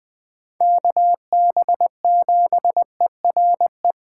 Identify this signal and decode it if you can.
Morse code, heard as a keyed tone, about 20 wpm, 700 Hz. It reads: KB7ERE
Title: Speed Racing - USA 2x3 Call Signs 17wpm